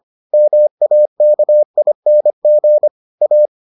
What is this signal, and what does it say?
Morse code, heard as a keyed tone, about 25 wpm, 600 Hz. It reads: MAKING A